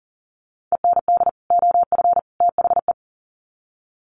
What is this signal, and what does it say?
Morse code, heard as a keyed tone, about 40 wpm, 700 Hz. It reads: END OF THE
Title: Sets of 3 Words 40wpm